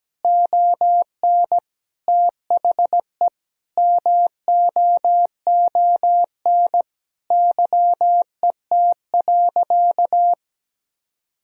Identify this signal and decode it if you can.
Morse code, heard as a keyed tone, about 17 wpm, 700 Hz. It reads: ON THE MOON YET.
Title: Sentences from Top 1000 Words - Review Part 3 - 17wpm